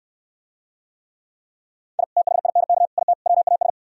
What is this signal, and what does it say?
Morse code, heard as a keyed tone, about 68 wpm, 700 Hz. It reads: I THINK IT WAS